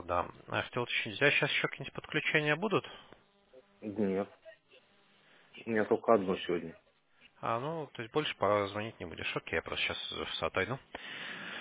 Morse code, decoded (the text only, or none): none